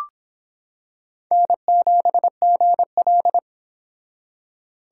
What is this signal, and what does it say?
Morse code, heard as a keyed tone, about 26 wpm, 700 Hz. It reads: N7GL